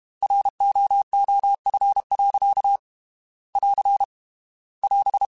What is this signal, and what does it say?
Morse code, heard as a keyed tone, about 32 wpm, 800 Hz. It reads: ROOF. <AR> <AS>
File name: Weather Sets / Weather Sets-mp3s / Weather 4 24 @30